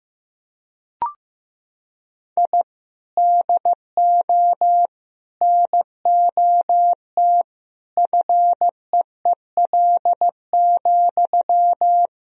Morse code, heard as a keyed tone, about 15 wpm, 700 Hz. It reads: I DO NOT FEEL,
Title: Sentences from Top 1000 Words - Review Part 3 - 15wpm